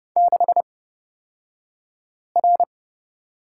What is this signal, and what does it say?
Morse code, heard as a keyed tone, about 30 wpm, spaced as for 5 wpm, 700 Hz. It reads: 6R
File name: Lesson 37d - Letter x - Two Characters 30wpm